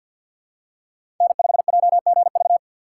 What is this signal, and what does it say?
Morse code, heard as a keyed tone, about 47 wpm, 690 Hz. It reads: N5JGV
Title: USA Call Signs - 2x Character Spacing - 50wpm